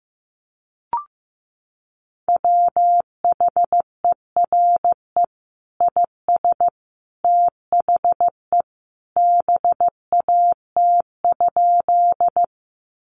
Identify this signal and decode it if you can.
Morse code, heard as a keyed tone, about 15 wpm, 700 Hz. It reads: WHERE IS THE BAT?
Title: Sentences from Top 1500 Words 15wpm